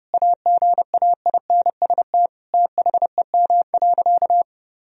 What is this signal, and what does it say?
Morse code, heard as a keyed tone, about 32 wpm, 700 Hz. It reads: AGAINST THEM.
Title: Sentences from Top 1000 Words - Review Part 4 - 30wpm